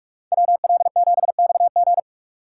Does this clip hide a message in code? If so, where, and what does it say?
Yes: Morse code from 0.3 to 2.0 s, heard as a keyed tone, about 41 wpm, 700 Hz. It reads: WL7XG